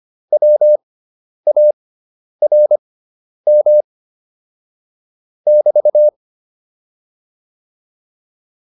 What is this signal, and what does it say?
Morse code, heard as a keyed tone, about 25 wpm, spaced as for 10 wpm, 600 Hz. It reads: WARM <BT>
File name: QSO301 10